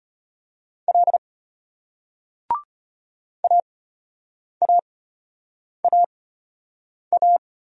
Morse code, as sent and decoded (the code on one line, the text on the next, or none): .-.. / .- .- .- .-
L AAAA